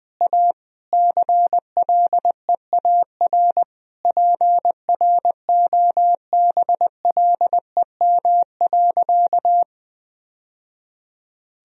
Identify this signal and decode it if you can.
Morse code, heard as a keyed tone, about 20 wpm, 700 Hz. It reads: A CLEAR PROBLEM.
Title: Sentences from Top 1000 Words - Review Part 2 - 20wpm